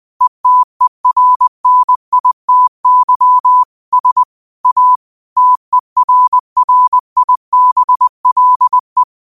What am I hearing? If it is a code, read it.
Morse code, heard as a keyed tone, about 20 wpm, 1000 Hz. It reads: ETERNITY S A TERRIBLE
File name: qotdfiles / 20wpm